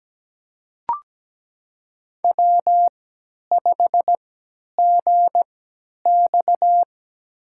Morse code, heard as a keyed tone, about 17 wpm, spaced as for 10 wpm, 700 Hz. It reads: W5GX
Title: USA Call Signs - 4 Characters - 3x Character Spacing - 17wpm